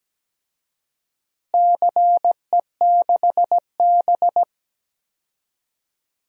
Morse code, heard as a keyed tone, about 17 wpm, 700 Hz. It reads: CE6B
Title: DXCC-100 Call Signs - 4 Characters - Repeated 2x - 17wpm